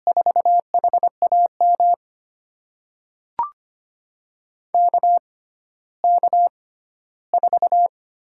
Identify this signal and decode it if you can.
Morse code, heard as a keyed tone, about 25 wpm, spaced as for 14 wpm, 700 Hz. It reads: 4HAM KK4
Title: USA Call Signs - 6 Characters - 6x Character Spacing - 25wpm